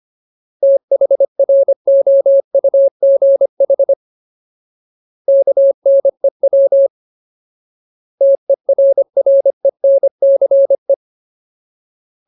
Morse code, heard as a keyed tone, about 25 wpm, 550 Hz. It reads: THROUGH KNEW TERRENCE
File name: Ultimate Word List / 25WPM Ultimate